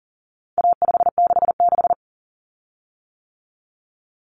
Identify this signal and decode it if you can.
Morse code, heard as a keyed tone, about 40 wpm, 700 Hz. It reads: A566
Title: Common Cut Numbers - 4-Digit 40wpm